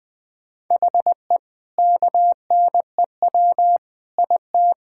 Morse code, heard as a keyed tone, about 20 wpm, 700 Hz. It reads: HE KNEW IT
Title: Sentences from Top 1000 Words - Review Part 6 - 20wpm